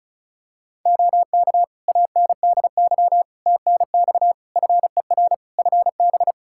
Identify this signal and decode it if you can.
Morse code, heard as a keyed tone, about 36 wpm, 700 Hz. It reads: OK ANDY TNX FER FB
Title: K1USN SST QSO Practice 35wpm